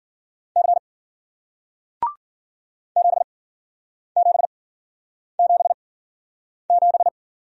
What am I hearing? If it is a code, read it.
Morse code, heard as a keyed tone, about 54 wpm, spaced as for 9 wpm, 700 Hz. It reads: C 7777